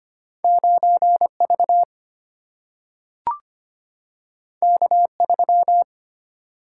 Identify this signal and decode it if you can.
Morse code, heard as a keyed tone, about 25 wpm, 700 Hz. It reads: 9V K3
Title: Single-Letter Single-Number 25wpm